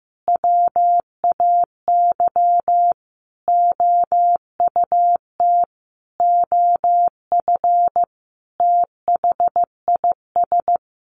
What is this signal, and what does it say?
Morse code, heard as a keyed tone, about 15 wpm, 700 Hz. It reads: WAY OUT OF THIS
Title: Sentences from Top 1200 Words 15wpm